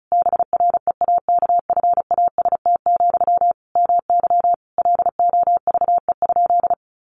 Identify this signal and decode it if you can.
Morse code, heard as a keyed tone, about 34 wpm, 700 Hz. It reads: BREAKFAST, MY LOVE?
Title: Sentences from Top 1500 Words 35wpm